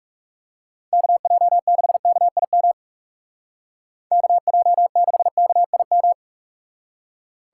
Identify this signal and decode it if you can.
Morse code, heard as a keyed tone, about 41 wpm, 700 Hz. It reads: KJ6KIM KJ6KIM